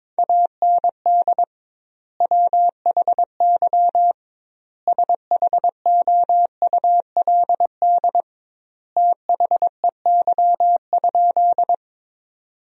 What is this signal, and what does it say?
Morse code, heard as a keyed tone, about 22 wpm, 700 Hz. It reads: AND WHY SHOULD THEY?
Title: 2x Word-Spacing - Sentences from Top 300 Words - 22wpm